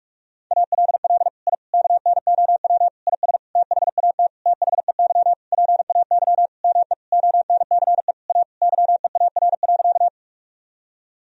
Morse code, heard as a keyed tone, about 46 wpm, 700 Hz. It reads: ALL I KNOW IS THAT THEY PAY ME ONCE A YEAR.
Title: Sentences from Top 1000 Words - Review Part 1 - 45wpm